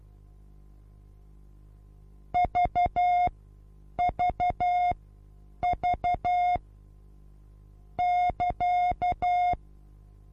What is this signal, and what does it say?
Morse code, heard as a keyed tone, about 12 wpm, spaced as for 8 wpm, 720 Hz. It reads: VVV <KA>
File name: Onlinelehrgang / morsen / DJ4UF Morsen23